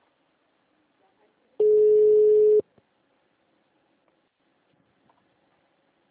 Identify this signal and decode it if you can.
Ringback tone (Europe, 425 Hz).
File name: out- -2007- - - .43816